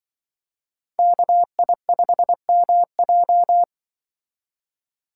Morse code, heard as a keyed tone, about 24 wpm, 700 Hz. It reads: KI5MJ
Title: Speed Racing - USA 2x2 Call Signs 20wpm